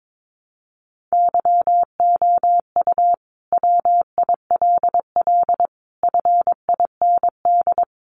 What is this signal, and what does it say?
Morse code, heard as a keyed tone, about 22 wpm, 700 Hz. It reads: YOU WILL FIND
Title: Sentences from Top 700 Words 22wpm